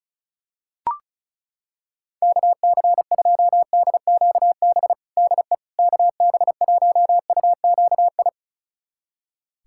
Morse code, heard as a keyed tone, about 36 wpm, 700 Hz. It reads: KC2DQB DE KB1UQI
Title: USA Callsign DE Callsign 35wpm